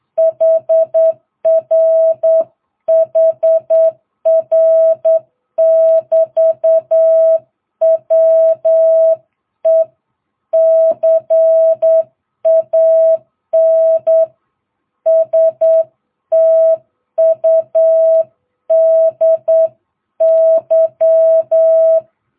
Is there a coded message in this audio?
Yes: Morse code from 0.2 to 22.0 s, heard as a keyed tone, about 8 wpm, 660 Hz. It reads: HR HR <BT> WE CAN STUDY